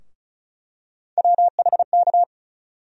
Morse code, heard as a keyed tone, about 36 wpm, 700 Hz. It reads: WHK
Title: USA Call Sign Postfixes - 3 Letters 35wpm